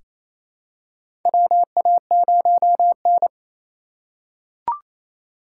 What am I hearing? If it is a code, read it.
Morse code, heard as a keyed tone, about 27 wpm, 700 Hz. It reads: WA0N
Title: USA Call Signs - 4 Characters - 4x Character Spacing - 28wpm